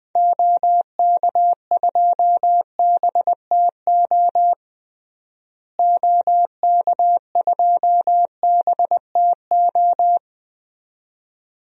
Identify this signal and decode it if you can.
Morse code, heard as a keyed tone, about 20 wpm, 700 Hz. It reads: OK2BTO OK2BTO
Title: DXCC-100 Call Signs - 6 Characters - Repeated 2x - 20wpm